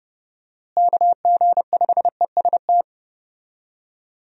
Morse code, heard as a keyed tone, about 31 wpm, 700 Hz. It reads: KG5EST
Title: USA Call Signs - 6 Characters - Repeated 3x - 30wpm